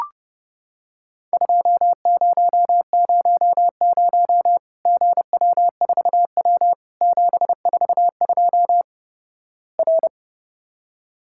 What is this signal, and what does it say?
Morse code, heard as a keyed tone, about 32 wpm, 700 Hz. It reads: 2000 GW4W 742 R
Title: Worked All Europe DX Contest - QTC Exchange - 30wpm